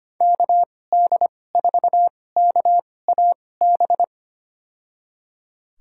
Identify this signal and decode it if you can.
Morse code, heard as a keyed tone, about 25 wpm, spaced as for 18 wpm, 700 Hz. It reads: KD4KAB
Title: USA Call Signs - 6 Characters - 2x Character Spacing - 25wpm